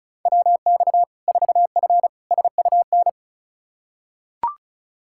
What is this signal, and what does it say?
Morse code, heard as a keyed tone, about 34 wpm, 700 Hz. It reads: WX 4F SUN
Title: Practice QSO - Exchange 2 - Info - 35wpm